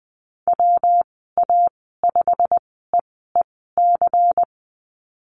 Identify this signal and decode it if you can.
Morse code, heard as a keyed tone, about 20 wpm, spaced as for 14 wpm, 700 Hz. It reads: WA5EEC
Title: 128 Lesson 21g - Number 6 - Call Signs 20wpm